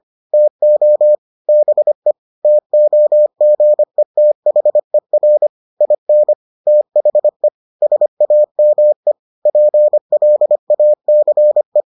Morse code, heard as a keyed tone, about 25 wpm, 600 Hz. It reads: TO BE TOGETHER IN THE SAME PLACE